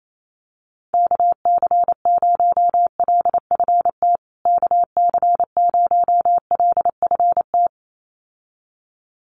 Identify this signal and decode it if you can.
Morse code, heard as a keyed tone, about 28 wpm, 700 Hz. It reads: KC0LFT KC0LFT
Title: USA Call Signs Repeated 2x 28wpm